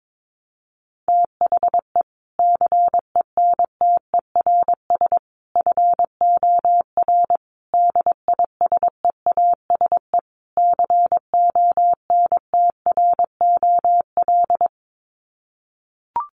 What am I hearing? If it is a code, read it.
Morse code, heard as a keyed tone, about 22 wpm, 700 Hz. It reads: THE CENTERS FOR DISEASE CONTROL
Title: Sets of 5 Words Encore2 22wpm